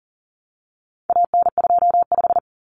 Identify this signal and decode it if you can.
Morse code, heard as a keyed tone, about 40 wpm, 700 Hz. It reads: AN25
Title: Common Cut Numbers - 4-Digit 40wpm